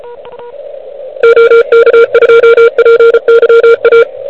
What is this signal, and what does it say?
Morse code, heard as a keyed tone, about 33 wpm, 1390 Hz, with noise. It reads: OK2PYA
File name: ARCHIV / QCX / QCX Pileup